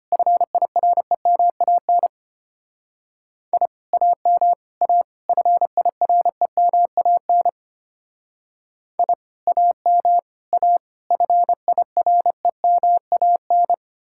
Morse code, heard as a keyed tone, about 32 wpm, 700 Hz. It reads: FIREMAN I AM A FIREMAN I AM A FIREMAN